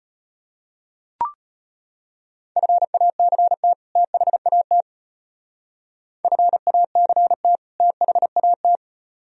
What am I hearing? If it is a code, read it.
Morse code, heard as a keyed tone, about 32 wpm, 700 Hz. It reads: FACT THAT FACT THAT